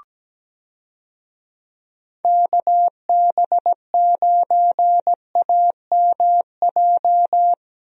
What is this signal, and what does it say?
Morse code, heard as a keyed tone, about 17 wpm, 700 Hz. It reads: KB9AMJ
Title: USA Call Signs - 6 Characters - Repeated 3x - 17wpm